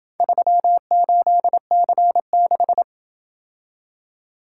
Morse code, heard as a keyed tone, about 28 wpm, 700 Hz. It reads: 38C6